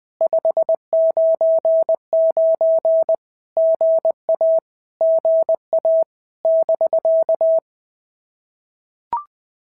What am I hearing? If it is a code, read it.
Morse code, heard as a keyed tone, about 20 wpm, 650 Hz. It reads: 599 GA GA <BK>
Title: POTA QSO Practice - 20wpm